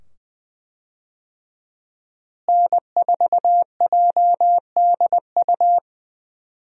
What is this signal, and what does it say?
Morse code, heard as a keyed tone, about 20 wpm, 700 Hz. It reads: N4JDU
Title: USA Call Signs - 5 Characters - Repeated 2x - 20wpm